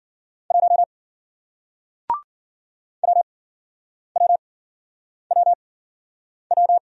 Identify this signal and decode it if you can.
Morse code, heard as a keyed tone, about 53 wpm, spaced as for 9 wpm, 700 Hz. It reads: 1 WWWW